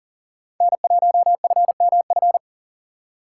Morse code, heard as a keyed tone, about 54 wpm, 700 Hz. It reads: N1FMF